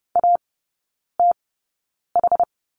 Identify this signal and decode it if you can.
Morse code, heard as a keyed tone, about 30 wpm, spaced as for 9 wpm, 700 Hz. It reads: ATH
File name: course / 30:10wpm / 087 Lesson 15e - Number 5 - Three Characters 30wpm